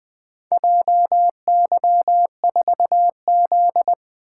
Morse code, heard as a keyed tone, about 20 wpm, 700 Hz. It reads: JY4Z
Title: Four Letter-Number Combinations - 20wpm